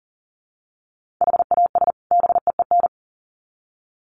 Morse code, heard as a keyed tone, about 40 wpm, 700 Hz. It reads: HAS BEEN
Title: Sets of 2 Words 40wpm